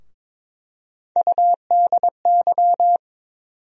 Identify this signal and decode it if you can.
Morse code, heard as a keyed tone, about 22 wpm, 700 Hz. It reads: UDY
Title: USA Call Sign Postfixes - 3 Letters 22wpm